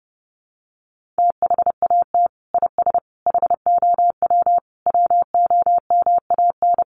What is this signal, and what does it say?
Morse code, heard as a keyed tone, about 30 wpm, 700 Hz. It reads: THAT IS HOW WOMAN
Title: Sentences from Top 1100 Words 30wpm